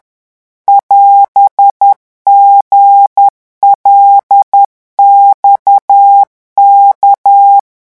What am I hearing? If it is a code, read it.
Morse code, heard as a keyed tone, about 10 wpm, 800 Hz. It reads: <AS>GLXK
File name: morse mp3 / revision16-vitesse 10 mots